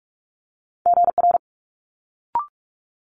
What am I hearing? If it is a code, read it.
Morse code, heard as a keyed tone, about 46 wpm, 700 Hz. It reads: GR
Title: Multi-letter Phonemes Quick Fire 45wpm